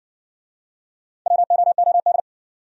Morse code, heard as a keyed tone, about 58 wpm, 700 Hz. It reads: WOOD